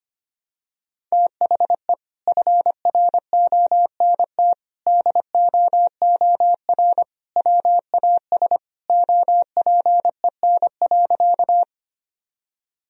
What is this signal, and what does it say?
Morse code, heard as a keyed tone, about 25 wpm, 700 Hz. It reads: THE FRONT DOOR WAS OPEN.
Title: Sentences from Top 1000 Words - Review Part 2 - 25wpm